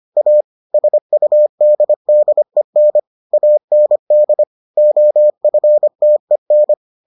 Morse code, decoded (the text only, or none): A SUDDEN AND OFTEN